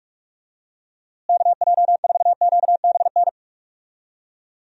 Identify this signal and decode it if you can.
Morse code, heard as a keyed tone, about 46 wpm, 700 Hz. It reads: KJ4QBN
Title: USA Call Signs - 6 Characters - Repeated 2x - 45wpm